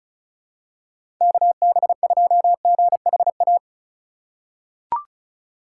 Morse code, heard as a keyed tone, about 36 wpm, 700 Hz. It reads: KB2GHA
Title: USA Call Signs - 4x Character Spacing - 35wpm